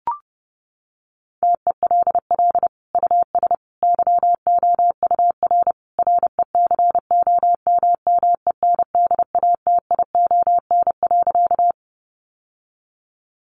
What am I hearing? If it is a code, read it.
Morse code, heard as a keyed tone, about 30 wpm, 700 Hz. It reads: TELL US YOUR RECOMMENDATION.
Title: Sentences from Top 1400 Words 30wpm